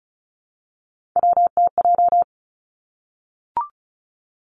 Morse code, heard as a keyed tone, about 36 wpm, 700 Hz. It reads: WTJ
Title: Three Challenging Letter-Number Combinations 35wpm